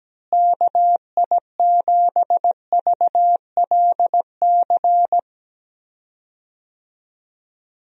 Morse code, heard as a keyed tone, about 17 wpm, 700 Hz. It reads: KI7VLC
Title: USA Call Signs - 6 Characters - Repeated 3x - 17wpm